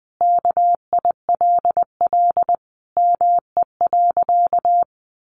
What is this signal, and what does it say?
Morse code, heard as a keyed tone, about 20 wpm, 700 Hz. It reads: KILL ME.
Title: Sentences from Top 1400 Words 20wpm